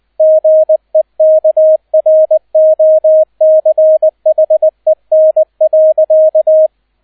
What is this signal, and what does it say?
Morse code, heard as a keyed tone, about 20 wpm, 630 Hz, with radio noise. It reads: GEKROCHEN.